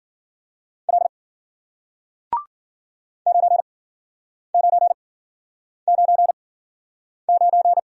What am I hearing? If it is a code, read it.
Morse code, heard as a keyed tone, about 50 wpm, spaced as for 9 wpm, 700 Hz. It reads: L 9999
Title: Speed-Racing Single Letter-Number 40wpm